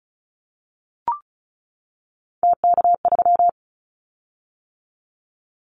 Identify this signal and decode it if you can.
Morse code, heard as a keyed tone, about 34 wpm, 700 Hz. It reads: TK3
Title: Callsign Prefixes for Top 100 DXCC - 35wpm